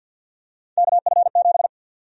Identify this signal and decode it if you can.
Morse code, heard as a keyed tone, about 53 wpm, 700 Hz. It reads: KF7
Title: USA Call Sign Prefixes - 50wpm